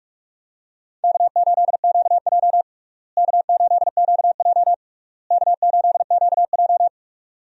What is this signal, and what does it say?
Morse code, heard as a keyed tone, about 41 wpm, 700 Hz. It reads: K8QJ K8QJ K8QJ
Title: USA Call Signs - 4 Characters - Repeated 3x - 45wpm